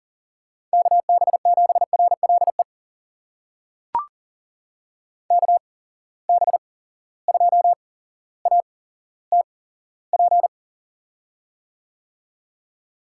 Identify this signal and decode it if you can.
Morse code, heard as a keyed tone, about 40 wpm, spaced as for 17 wpm, 700 Hz. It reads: KB7RLE KB2ATP
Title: USA Call Signs - 6 Characters - 8x Character Spacing - 40wpm